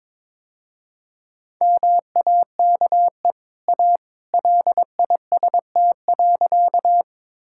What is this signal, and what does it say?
Morse code, heard as a keyed tone, about 22 wpm, 690 Hz. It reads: MAKE A LIST.
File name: practice / 22wpm / Easy Sentences with Hints - 22wpm